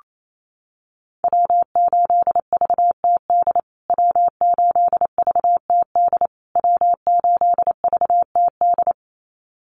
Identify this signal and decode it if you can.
Morse code, heard as a keyed tone, about 28 wpm, 700 Hz. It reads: W8VTD W8VTD W8VTD
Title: USA Call Signs Repeated 3x 28wpm